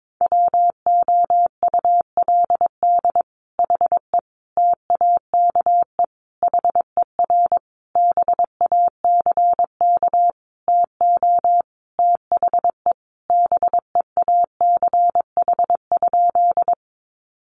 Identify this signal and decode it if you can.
Morse code, heard as a keyed tone, about 22 wpm, 700 Hz. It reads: WOULD HE TAKE HER BACK TO THE BEACH?
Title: Sentences from Top 1500 Words 22wpm